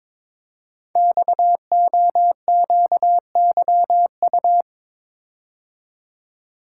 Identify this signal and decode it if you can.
Morse code, heard as a keyed tone, about 22 wpm, 700 Hz. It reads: XOQYU
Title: Code Groups - 5 Letters - 22wpm